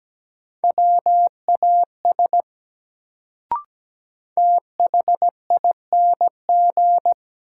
Morse code, heard as a keyed tone, about 17 wpm, 700 Hz. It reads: WAS THING
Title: Sentences from Top 100 Words - New Word Drill - 17wpm